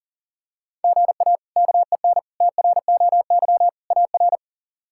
Morse code, heard as a keyed tone, about 37 wpm, 700 Hz. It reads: GA KEN TROY AR
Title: K1USN SST QSO Practice - 40wpm